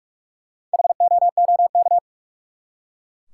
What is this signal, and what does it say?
Morse code, heard as a keyed tone, about 43 wpm, 700 Hz. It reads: HOOK
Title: Four Letter Words - 45wpm